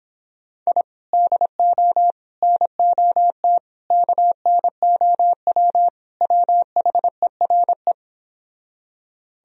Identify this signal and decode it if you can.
Morse code, heard as a keyed tone, about 26 wpm, 700 Hz. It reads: I DO NOT KNOW WHERE